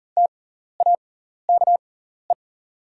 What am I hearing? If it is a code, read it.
Morse code, heard as a keyed tone, about 42 wpm, spaced as for 14 wpm, 690 Hz. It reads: TAKE
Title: 6x Character Spacing - Top 200 Words - 40wpm